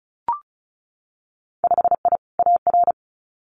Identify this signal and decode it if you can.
Morse code, heard as a keyed tone, about 36 wpm, 700 Hz. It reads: 5I AR
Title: Winter Field Day Exchanges Encore 35wpm